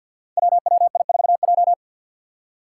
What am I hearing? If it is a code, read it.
Morse code, heard as a keyed tone, about 52 wpm, 700 Hz. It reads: WWI4J
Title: Five Challenging Letter-Number Combinations 50wpm